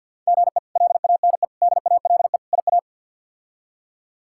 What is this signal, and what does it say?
Morse code, heard as a keyed tone, about 50 wpm, 700 Hz. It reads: GE LANE DALE IA
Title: K1USN SST QSO Practice - 50wpm